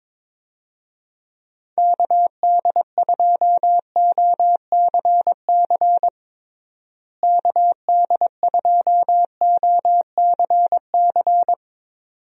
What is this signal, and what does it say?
Morse code, heard as a keyed tone, about 22 wpm, 700 Hz. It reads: KD2OCC KD2OCC
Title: USA Call Signs - 6 Characters - Repeated 2x - 22wpm